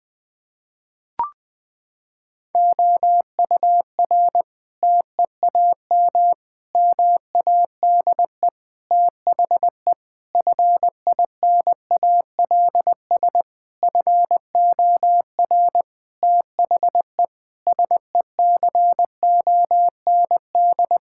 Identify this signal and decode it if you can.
Morse code, heard as a keyed tone, about 21 wpm, 700 Hz. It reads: OUR TEAM MADE THE FINALS FOR THE SECOND
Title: Sentences with 7-Word Phrases - 20wpm